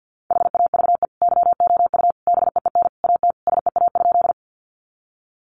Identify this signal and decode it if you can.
Morse code, heard as a keyed tone, about 50 wpm, 700 Hz. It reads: HAVE YOU BEEN AT SEA?
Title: Sentences from Top 300 Words 50wpm